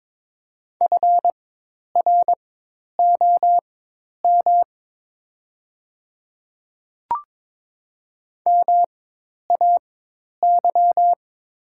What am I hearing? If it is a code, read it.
Morse code, heard as a keyed tone, about 22 wpm, spaced as for 10 wpm, 700 Hz. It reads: FROM MAY
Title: 4x Character Spacing - Top 500 Words - 22wpm